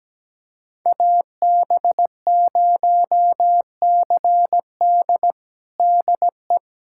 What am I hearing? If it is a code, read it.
Morse code, heard as a keyed tone, about 17 wpm, 700 Hz. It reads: AB0CD DE
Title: USA Callsign DE Callsign 17wpm